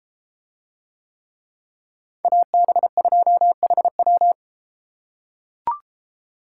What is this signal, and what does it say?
Morse code, heard as a keyed tone, about 31 wpm, 700 Hz. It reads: AB2HW